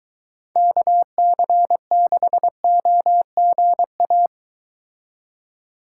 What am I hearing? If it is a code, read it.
Morse code, heard as a keyed tone, about 22 wpm, 700 Hz. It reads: KC6OGA